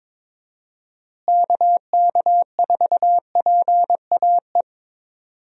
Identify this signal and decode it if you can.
Morse code, heard as a keyed tone, about 22 wpm, 700 Hz. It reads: KK4PAE
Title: USA Call Signs - 6 Characters - Repeated 3x - 22wpm